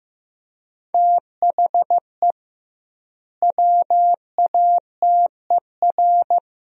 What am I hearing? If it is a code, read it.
Morse code, heard as a keyed tone, about 16 wpm, 700 Hz. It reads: THE WATER
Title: 2x Word-Spacing - Sentences from Top 500 Words - 15wpm